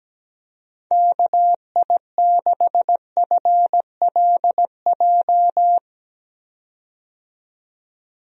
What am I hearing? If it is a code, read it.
Morse code, heard as a keyed tone, about 17 wpm, 700 Hz. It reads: KI6FLJ
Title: USA Call Signs - 6 Characters - 17wpm